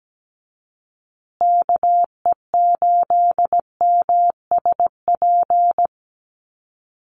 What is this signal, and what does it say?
Morse code, heard as a keyed tone, about 17 wpm, 700 Hz. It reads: KE8MSP
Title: USA Call Signs Repeated 2x 17wpm